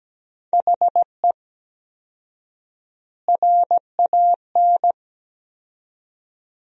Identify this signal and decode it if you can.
Morse code, heard as a keyed tone, about 17 wpm, 700 Hz. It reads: HE RAN
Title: 4x Word-Spacing - Sentences from Top 500 Words - 17wpm